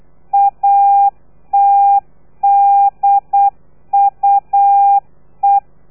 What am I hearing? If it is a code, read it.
Morse code, heard as a keyed tone, about 8 wpm, 800 Hz. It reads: ATDUE
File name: telegrafia / i2tfj / Lez3 35